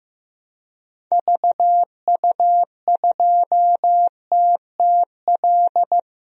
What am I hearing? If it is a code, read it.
Morse code, heard as a keyed tone, about 15 wpm, 700 Hz. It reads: VU2TTL
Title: DXCC-100 Call Signs - 6 Characters - 15wpm